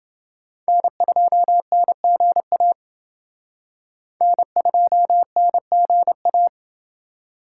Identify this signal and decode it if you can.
Morse code, heard as a keyed tone, about 28 wpm, 700 Hz. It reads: N2NGA N2NGA